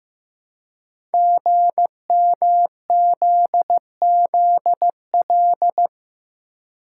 Morse code, heard as a keyed tone, about 15 wpm, 700 Hz. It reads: GMZZL
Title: Code Groups - 5 Letters - 15wpm